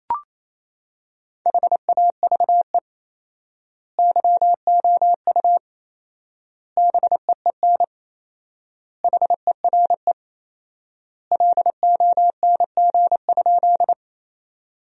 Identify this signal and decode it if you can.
Morse code, heard as a keyed tone, about 28 wpm, 700 Hz. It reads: HAVE YOU BEEN HERE LONG?
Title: 4x Word-Spacing - Sentences from Top 200 Words - 28wpm